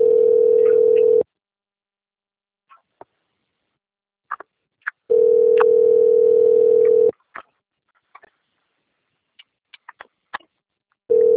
Ringback tone (North America), 440 + 480 Hz.
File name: out- - - - - .27557